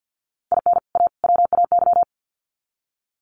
Morse code, heard as a keyed tone, about 50 wpm, 700 Hz. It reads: IN A WAY
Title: Sets of 3 Words 50wpm